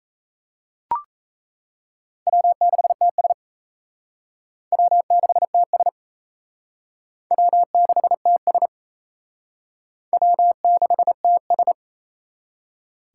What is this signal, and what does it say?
Morse code, heard as a keyed tone, about 37 wpm, 700 Hz. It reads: W6TS W6TS W6TS W6TS